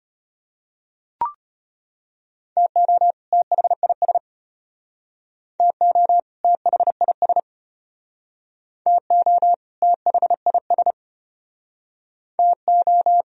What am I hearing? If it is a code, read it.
Morse code, heard as a keyed tone, about 33 wpm, 700 Hz. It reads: TO THIS TO THIS TO THIS TO